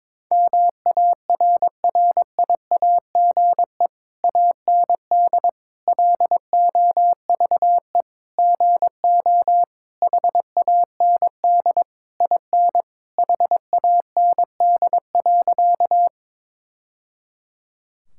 Morse code, heard as a keyed tone, about 22 wpm, 700 Hz. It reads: MARRIAGE AND LOVE GO HAND IN HAND.
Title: Sentences from Top 1000 Words - Review Part 5 - 22wpm